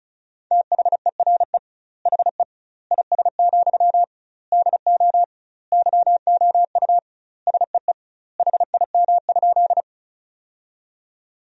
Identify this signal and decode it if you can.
Morse code, heard as a keyed tone, about 36 wpm, 700 Hz. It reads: THERE HE IS, DO YOU SEE HIM?
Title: 2x Word-Spacing - Sentences from Top 100 Words - 35wpm